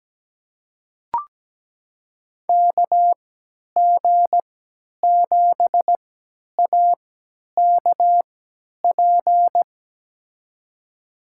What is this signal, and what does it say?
Morse code, heard as a keyed tone, about 17 wpm, spaced as for 10 wpm, 700 Hz. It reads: KG7AKP
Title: USA Call Signs - 6 Characters - 3x Character Spacing - 17wpm